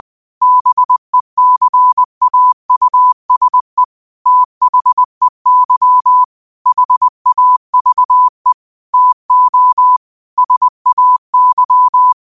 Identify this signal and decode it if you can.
Morse code, heard as a keyed tone, about 20 wpm, 1000 Hz. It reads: BECAUSE THEY HAVE TO SAY
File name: qotdfiles / 20wpm